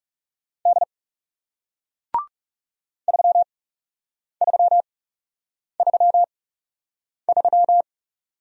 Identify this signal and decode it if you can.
Morse code, heard as a keyed tone, about 40 wpm, spaced as for 8 wpm, 700 Hz. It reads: D 3333